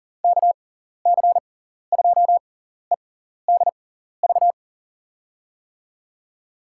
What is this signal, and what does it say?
Morse code, heard as a keyed tone, about 40 wpm, spaced as for 14 wpm, 700 Hz. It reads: KC2EDV
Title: USA Call Signs - 6 Characters - 6x Character Spacing - 40wpm